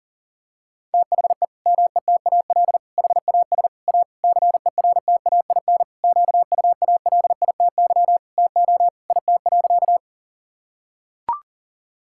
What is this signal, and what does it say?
Morse code, heard as a keyed tone, about 40 wpm, 700 Hz. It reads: THE METAL HAS A CERTAIN QUALITY TO IT.
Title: Sentences from Top 1000 Words - Review Part 6 - 40wpm